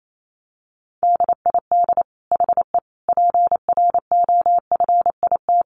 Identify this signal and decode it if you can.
Morse code, heard as a keyed tone, about 28 wpm, 700 Hz. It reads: DID HE PROFIT
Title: Sentences from Top 800 Words 28wpm